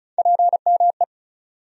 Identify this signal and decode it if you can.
Morse code, heard as a keyed tone, about 36 wpm, 700 Hz. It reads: PME
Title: USA Call Sign Postfixes - 3 Letters 35wpm